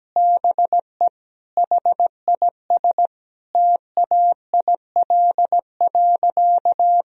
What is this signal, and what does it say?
Morse code, heard as a keyed tone, about 17 wpm, 700 Hz. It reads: BE HIS TAIL.
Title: Sentences from Top 1000 Words - Review Part 3 - 17wpm